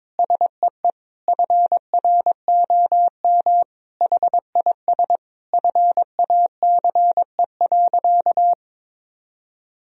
Morse code, heard as a keyed tone, about 22 wpm, 700 Hz. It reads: SEE FROM HIS FACE.
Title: Sentences from Top 1000 Words - Review Part 2 - 22wpm